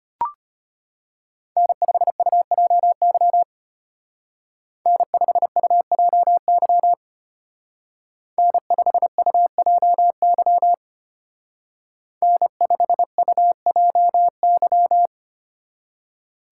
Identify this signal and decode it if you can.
Morse code, heard as a keyed tone, about 31 wpm, 700 Hz. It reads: N5UJY N5UJY N5UJY N5UJY